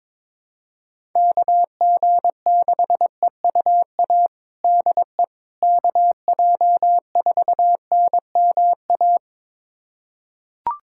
Morse code, heard as a keyed tone, about 21 wpm, 700 Hz. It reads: KG6EUA DE KJ4NMA
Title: USA Callsign DE Callsign - 6 Characters - 22wpm